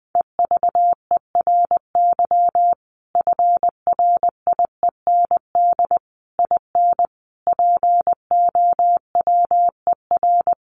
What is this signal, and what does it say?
Morse code, heard as a keyed tone, about 20 wpm, 700 Hz. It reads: EVERY FRIEND IN POWER
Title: Sentences from Top 1400 Words 20wpm